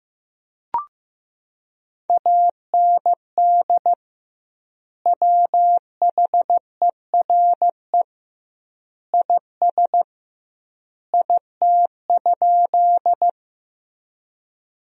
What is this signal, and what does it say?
Morse code, heard as a keyed tone, about 15 wpm, 700 Hz. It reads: AND WHERE IS IT?
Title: 2x Word-Spacing - Sentences from Top 200 Words - 15wpm